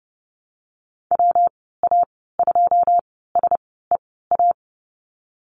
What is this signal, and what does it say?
Morse code, heard as a keyed tone, about 30 wpm, 700 Hz. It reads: W A 2 S E A
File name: Lesson 19g - Letter w - Call Signs 30wpm